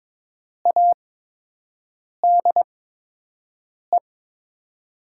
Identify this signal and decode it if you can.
Morse code, heard as a keyed tone, about 22 wpm, spaced as for 6 wpm, 700 Hz. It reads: ADE
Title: USA Call Signs - 6 Characters - 8x Character Spacing - 22wpm